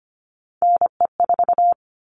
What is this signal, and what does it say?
Morse code, heard as a keyed tone, about 25 wpm, 700 Hz. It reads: NE4
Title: All Cut Numbers - 3-Digit 25wpm